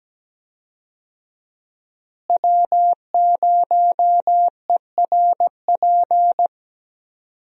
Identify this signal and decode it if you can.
Morse code, heard as a keyed tone, about 17 wpm, 700 Hz. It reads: W0ERP
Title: USA Call Signs - 5 Characters - Repeated 2x - 17wpm